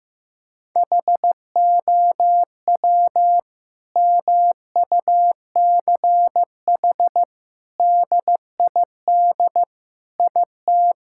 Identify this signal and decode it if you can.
Morse code, heard as a keyed tone, about 15 wpm, 700 Hz. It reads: HOW MUCH DID IT
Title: Sentences from Top 1000 Words - Review Part 5 - 15wpm